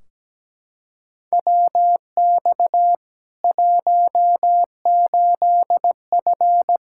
Morse code, heard as a keyed tone, about 17 wpm, 700 Hz. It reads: WX 18F
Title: Practice QSO - Exchange 2 - Info - 17wpm